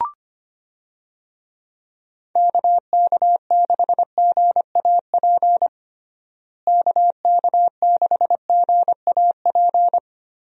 Morse code, heard as a keyed tone, about 25 wpm, 700 Hz. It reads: KK6GAP KK6GAP
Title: USA Call Signs - 6 Characters - Repeated 3x - 25wpm